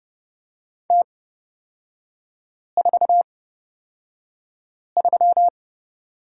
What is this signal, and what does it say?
Morse code, heard as a keyed tone, about 30 wpm, spaced as for 5 wpm, 700 Hz. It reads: T43
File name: Lesson 38e - Letter q - Three Characters 30wpm